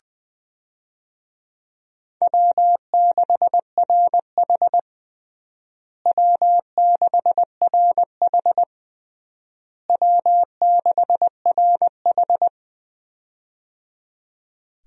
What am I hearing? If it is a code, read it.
Morse code, heard as a keyed tone, about 20 wpm, 700 Hz. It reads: W6RH W6RH W6RH
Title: USA Call Signs - 4 Characters - Repeated 3x - 20wpm